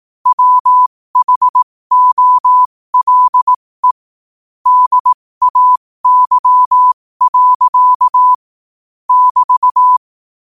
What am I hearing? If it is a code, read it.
Morse code, heard as a keyed tone, about 18 wpm, spaced as for 15 wpm, 1000 Hz. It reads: WHOLE DAY. <BT>
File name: qotdfiles / 15wpm